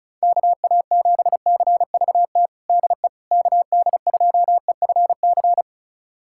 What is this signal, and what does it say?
Morse code, heard as a keyed tone, about 36 wpm, 700 Hz. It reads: KA7CVT DE KD2EFC
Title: USA Callsign DE Callsign 35wpm